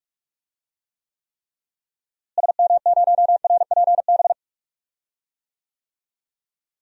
Morse code, heard as a keyed tone, about 41 wpm, 700 Hz. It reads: SM0RPB
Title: DXCC-100 Call Signs - 6 Characters - Repeated 3x - 45wpm